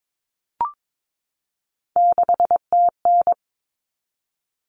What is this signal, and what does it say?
Morse code, heard as a keyed tone, about 22 wpm, 700 Hz. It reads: 6TN
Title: Common Cut Numbers - 3-Digit 22wpm